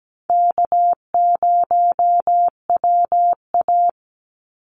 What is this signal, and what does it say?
Morse code, heard as a keyed tone, about 17 wpm, 700 Hz. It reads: K0WA
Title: Easy Call Signs - 17wpm